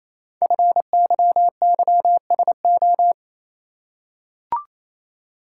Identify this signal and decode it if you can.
Morse code, heard as a keyed tone, about 27 wpm, 700 Hz. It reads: FYYSO
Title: Code Groups - 5 Letters - 28wpm